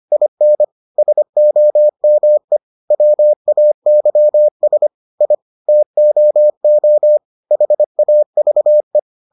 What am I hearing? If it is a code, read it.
Morse code, heard as a keyed tone, about 25 wpm, 600 Hz. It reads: IN SOME WAYS I TOO HAVE